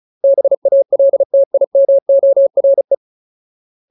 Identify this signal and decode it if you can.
Morse code, heard as a keyed tone, about 34 wpm, 550 Hz. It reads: BALTIMORE
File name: Ultimate Word List / 35WPM Ultimate